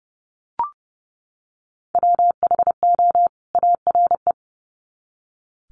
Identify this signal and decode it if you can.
Morse code, heard as a keyed tone, about 30 wpm, 700 Hz. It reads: WHO ARE
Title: Sets of 2 Words 30wpm